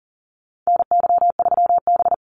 Morse code, heard as a keyed tone, about 40 wpm, 700 Hz. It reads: NY3B
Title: Call Signs - 40wpm